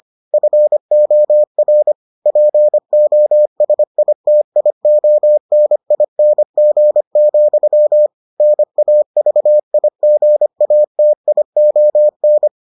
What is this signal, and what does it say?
Morse code, heard as a keyed tone, about 25 wpm, 600 Hz. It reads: FOR POSITIONING, NAVIGATION